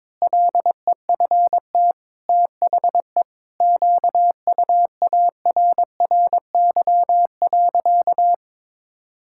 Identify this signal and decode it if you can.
Morse code, heard as a keyed tone, about 22 wpm, 700 Hz. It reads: LEFT THE QUARRY.
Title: Pangram Challenge v2 - 22wpm